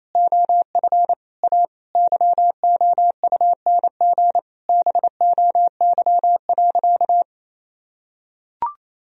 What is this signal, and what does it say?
Morse code, heard as a keyed tone, about 28 wpm, 700 Hz. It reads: OF A YOUNG BOY.
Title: Sentences from Top 1000 Words - Review Part 4 - 28wpm